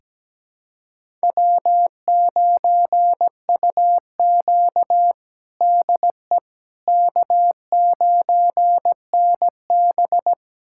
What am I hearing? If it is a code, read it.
Morse code, heard as a keyed tone, about 17 wpm, 700 Hz. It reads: W9UQ DE K9NB
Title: USA Callsign DE Callsign - 4 Characters - 17wpm